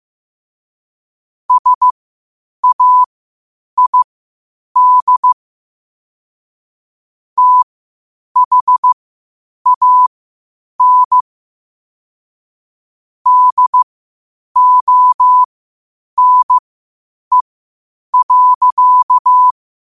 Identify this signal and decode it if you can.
Morse code, heard as a keyed tone, about 15 wpm, spaced as for 8 wpm, 1000 Hz. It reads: SAID THAN DONE.
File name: 8wpm